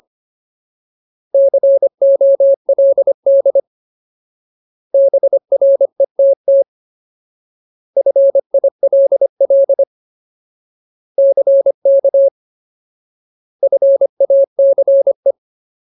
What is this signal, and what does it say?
Morse code, heard as a keyed tone, about 25 wpm, 550 Hz. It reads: COLD BRETT FILL CK FACE
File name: Ultimate Word List / 25WPM Ultimate